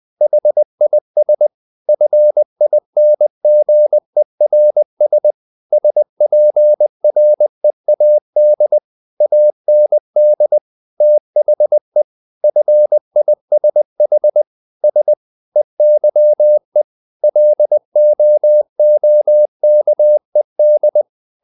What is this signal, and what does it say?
Morse code, heard as a keyed tone, about 20 wpm, 600 Hz. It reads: HIS FINGERS SPREAD AND THE FISH S EYE LOOKED
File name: the old man and the sea part 2 20wpm